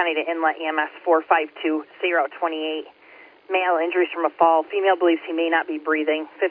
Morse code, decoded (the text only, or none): none